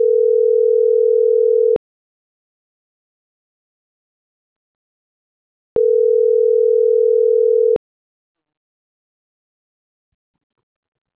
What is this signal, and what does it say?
Ringback tone (North America), 440 + 480 Hz.